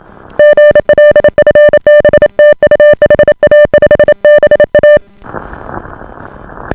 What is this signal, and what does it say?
Morse code, heard as a keyed tone, about 28 wpm, 590 Hz. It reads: GL FB TU HA5BA